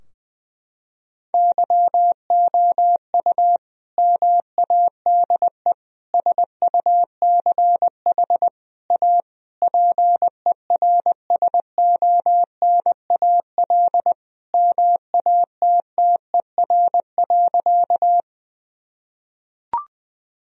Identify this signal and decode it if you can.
Morse code, heard as a keyed tone, about 20 wpm, 700 Hz. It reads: YOU MADE SUCH A PERSONAL MATTER.
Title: Sentences from Top 1000 Words - Review Part 4 - 20wpm